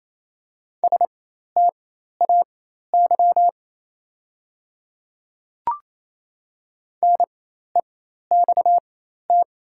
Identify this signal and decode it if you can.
Morse code, heard as a keyed tone, about 27 wpm, spaced as for 13 wpm, 700 Hz. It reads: STAY NEXT
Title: 4x Character Spacing - Top 500 Words - 28wpm